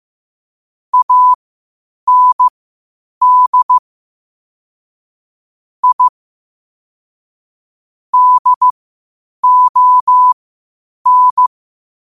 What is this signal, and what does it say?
Morse code, heard as a keyed tone, about 15 wpm, spaced as for 8 wpm, 1000 Hz. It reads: AND I DON
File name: qotdfiles / 8wpm